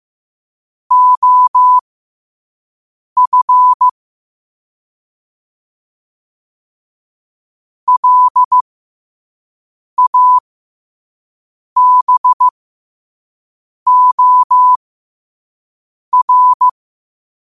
Morse code, heard as a keyed tone, about 15 wpm, spaced as for 5 wpm, 1000 Hz. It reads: OF LABOR
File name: qotdfiles / 5wpm